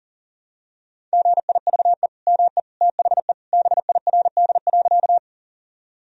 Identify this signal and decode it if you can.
Morse code, heard as a keyed tone, about 40 wpm, 700 Hz. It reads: GIVE ME THE BIRD.
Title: Sentences from Top 1000 Words - Review Part 2 - 40wpm